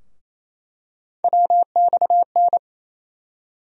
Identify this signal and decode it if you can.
Morse code, heard as a keyed tone, about 28 wpm, 700 Hz. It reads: WXN
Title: USA Call Sign Postfixes - 3 Letters 28wpm